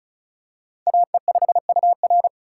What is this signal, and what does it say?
Morse code, heard as a keyed tone, about 34 wpm, 700 Hz. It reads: AE5UR
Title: USA Call Signs - 5 Characters - 2x Character Spacing - 35wpm